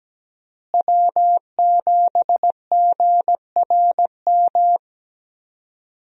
Morse code, heard as a keyed tone, about 17 wpm, 700 Hz. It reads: W7GRM
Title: USA Call Signs - 5 Characters - Repeated 2x - 17wpm